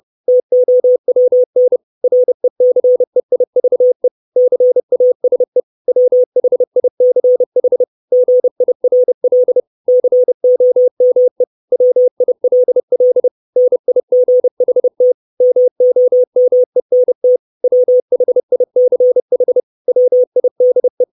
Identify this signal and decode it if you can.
Morse code, heard as a keyed tone, about 30 wpm, 500 Hz. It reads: TOWN RECEIVE CASE WHICH GIRL COME WILL NIGHT MOMENT WHICH WIDE
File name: 632 words / 632 Words 30WPM